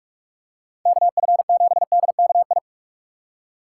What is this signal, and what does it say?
Morse code, heard as a keyed tone, about 40 wpm, 700 Hz. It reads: KF7DKI